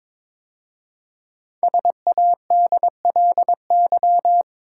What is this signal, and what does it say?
Morse code, heard as a keyed tone, about 22 wpm, 690 Hz. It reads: SADLY
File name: 5 Letter Words - First 3 Letters Given - 22wpm